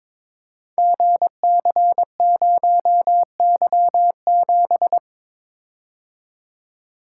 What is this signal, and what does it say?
Morse code, heard as a keyed tone, about 22 wpm, 700 Hz. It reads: GC0Y7
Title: Code Groups - 5 Letter:Numbers - 22wpm